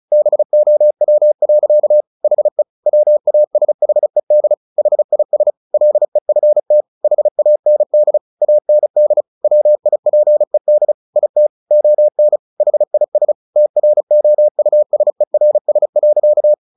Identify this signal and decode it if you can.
Morse code, heard as a keyed tone, about 35 wpm, 600 Hz. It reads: BOW. HE WASHED HIS LEFT HAND AND WIPED IT ON HIS TROUSERS.
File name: the old man and the sea 2 35wpm